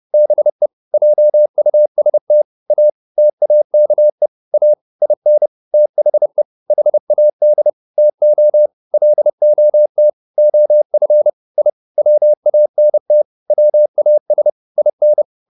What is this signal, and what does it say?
Morse code, heard as a keyed tone, about 30 wpm, 600 Hz. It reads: BE JUST A TAKE A IN THE HAD TO LOT OF I WANT WAS IN
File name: WD104 30